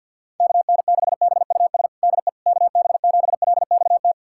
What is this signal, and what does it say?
Morse code, heard as a keyed tone, about 47 wpm, 700 Hz. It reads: KN6BUS DE KB7LXT
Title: USA Callsign DE Callsign - 6 Characters - 50wpm